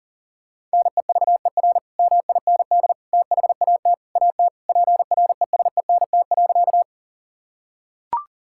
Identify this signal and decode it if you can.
Morse code, heard as a keyed tone, about 37 wpm, 700 Hz. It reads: NEVER MIND THAT AT PRESENT.
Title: Sentences from Top 1000 Words - Review Part 5 - 40wpm